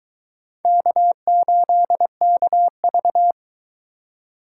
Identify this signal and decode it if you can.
Morse code, heard as a keyed tone, about 22 wpm, 700 Hz. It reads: K8KV